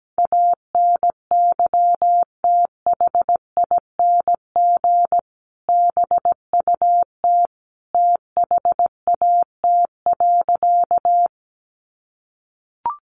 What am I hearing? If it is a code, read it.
Morse code, heard as a keyed tone, about 17 wpm, 700 Hz. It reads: ANYTHING BUT THAT.
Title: Sentences from Top 1000 Words 17wpm